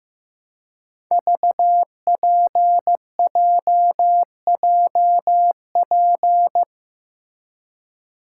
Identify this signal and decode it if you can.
Morse code, heard as a keyed tone, about 15 wpm, 700 Hz. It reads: VPJJP
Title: Code Groups - 5 Letters - 15wpm